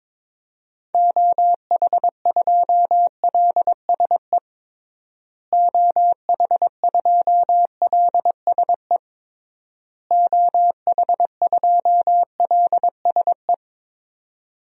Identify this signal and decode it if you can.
Morse code, heard as a keyed tone, about 22 wpm, 700 Hz. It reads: OH2LSE OH2LSE OH2LSE
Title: DXCC-100 Call Signs - 6 Characters - Repeated 3x - 22wpm